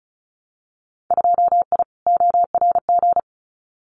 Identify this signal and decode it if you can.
Morse code, heard as a keyed tone, about 34 wpm, 700 Hz. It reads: 2I ORG
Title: Winter Field Day Exchanges Encore 35wpm